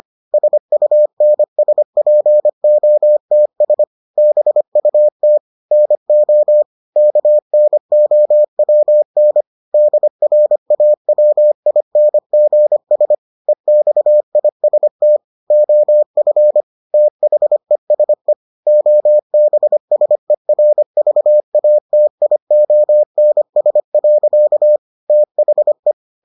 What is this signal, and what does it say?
Morse code, heard as a keyed tone, about 25 wpm, 600 Hz. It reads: SUNSPOTS BUT NO KNOWN DRAWINGS EXIST OF THESE OBSERVATIONS. THE